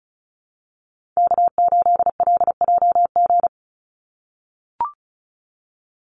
Morse code, heard as a keyed tone, about 34 wpm, 700 Hz. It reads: K8LJG
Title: NA SOTA Chaser Callsigns 35wpm